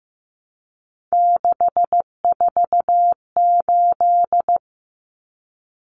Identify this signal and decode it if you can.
Morse code, heard as a keyed tone, about 15 wpm, 700 Hz. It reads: 648
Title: Numbers - 3-Digit 15wpm